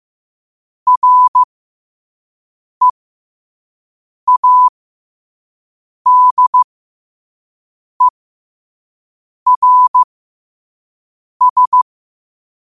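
Morse code, heard as a keyed tone, about 15 wpm, spaced as for 5 wpm, 1000 Hz. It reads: READERS
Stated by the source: Morse code practice at 5 wpm.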